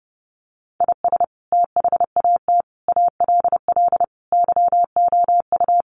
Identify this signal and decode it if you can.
Morse code, heard as a keyed tone, about 30 wpm, 700 Hz. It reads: IS THAT ALL YOU
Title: Sentences from Top 200 Words 30wpm